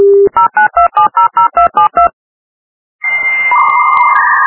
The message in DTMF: 092700272